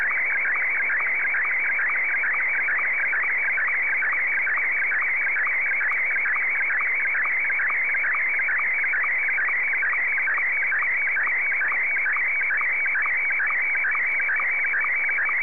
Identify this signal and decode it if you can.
SSTV (slow-scan television), Martin 1.
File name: sstv / M1-VK3